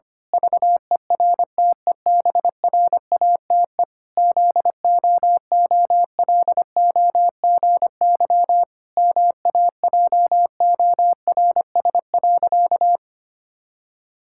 Morse code, heard as a keyed tone, about 25 wpm, 700 Hz. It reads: VERTEBRATE ZOOLOGY MAJORS.
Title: Pangram Challenge v2 - 25wpm